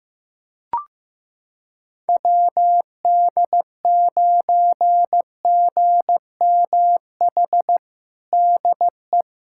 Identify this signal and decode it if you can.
Morse code, heard as a keyed tone, about 15 wpm, 700 Hz. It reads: WD9GMH DE
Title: USA Callsign DE Callsign 15wpm